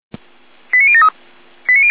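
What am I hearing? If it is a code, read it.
Five-tone selective call (ZVEI), heard as quick runs of five tones, tones of about 70 ms. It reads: 89073
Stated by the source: ZVEI five-tone selective call 89130 received on SDR.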